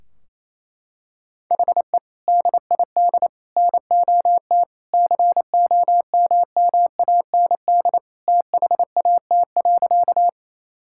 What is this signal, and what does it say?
Morse code, heard as a keyed tone, about 28 wpm, 700 Hz. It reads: HE DID NOT COMMAND THAT.
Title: Sentences from Top 1600 Words 28wpm